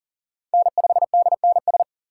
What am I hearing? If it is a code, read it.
Morse code, heard as a keyed tone, about 40 wpm, 700 Hz. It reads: N5DNS
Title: USA Call Signs - 5 Characters - 3x Character Spacing - 40wpm